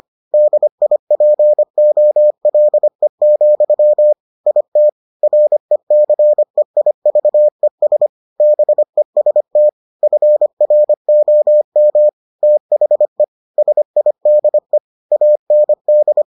Morse code, heard as a keyed tone, about 25 wpm, 600 Hz. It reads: DIPOLE, IT RECEIVES BEST FROM THE SIDE AND